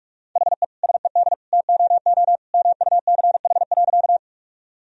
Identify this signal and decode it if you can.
Morse code, heard as a keyed tone, about 46 wpm, 700 Hz. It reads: HE SED TOO MUCH.
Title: Sentences from Top 200 Words and Abbreviations 45wpm